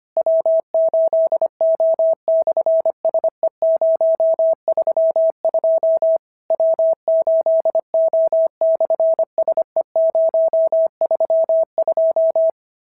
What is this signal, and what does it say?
Morse code, heard as a keyed tone, about 25 wpm, 650 Hz. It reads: W8O/SE032 W8O/SE032
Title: SOTA QSO Practice 25wpm